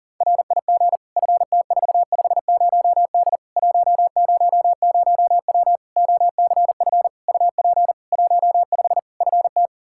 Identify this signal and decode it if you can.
Morse code, heard as a keyed tone, about 40 wpm, 700 Hz. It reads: RIG FT450D 100W OCF UP 15 FT
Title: Practice QSO - Exchange 2 - Info - 40wpm